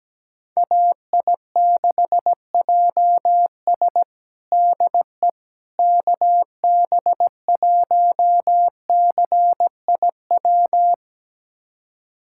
Morse code, heard as a keyed tone, about 17 wpm, 700 Hz. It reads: AI6JS DE KB1CIW
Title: USA Callsign DE Callsign 17wpm